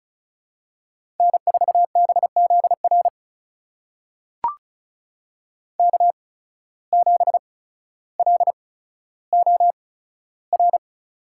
Morse code, heard as a keyed tone, about 36 wpm, spaced as for 15 wpm, 700 Hz. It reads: N4BZR K7LOR
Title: USA Call Signs - 5 Characters - 8x Character Spacing - 35wpm